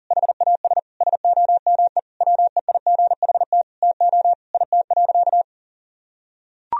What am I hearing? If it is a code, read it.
Morse code, heard as a keyed tone, about 40 wpm, 700 Hz. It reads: HAS SOME WEIGHT TO IT.
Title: Sentences from Top 1000 Words - Review Part 3 - 40wpm